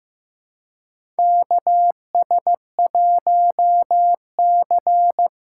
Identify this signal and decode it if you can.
Morse code, heard as a keyed tone, about 15 wpm, 700 Hz. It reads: KS1C
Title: USA Call Signs - 4 Characters - 2x Character Spacing - 15wpm